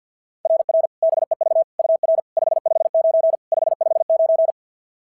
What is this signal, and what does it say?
Morse code, heard as a keyed tone, about 49 wpm, 660 Hz. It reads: RR BEV UR 559 559
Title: SOTA QSO Practice 50wpm